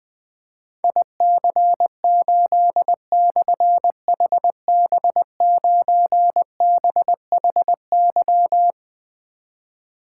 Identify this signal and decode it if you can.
Morse code, heard as a keyed tone, about 20 wpm, 700 Hz. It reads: IC8/HB9BHY
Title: Difficult Call Signs 20wpm